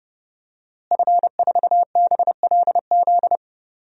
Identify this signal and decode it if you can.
Morse code, heard as a keyed tone, about 30 wpm, 700 Hz. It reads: F4BLZ
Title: DXCC-100 Call Signs - 5 Characters - Repeated 3x - 30wpm